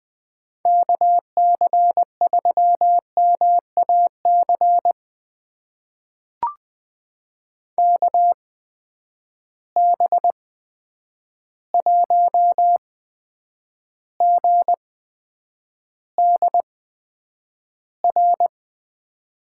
Morse code, heard as a keyed tone, about 20 wpm, spaced as for 9 wpm, 700 Hz. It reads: KC3MAC KB1GDR
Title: USA Call Signs - 8x Character Spacing - 20wpm